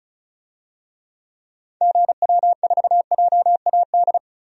Morse code, heard as a keyed tone, about 36 wpm, 700 Hz. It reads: GW4JAD
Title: DXCC-100 Call Signs - 6 Characters - 35wpm